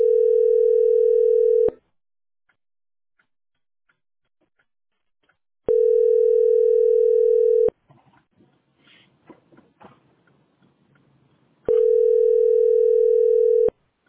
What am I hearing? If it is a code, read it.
Ringback tone (North America), 440 + 480 Hz.